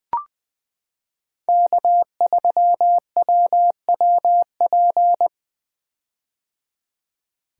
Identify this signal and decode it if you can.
Morse code, heard as a keyed tone, about 20 wpm, 700 Hz. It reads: K3WWP
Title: K1USN SST QSO Practice - 20wpm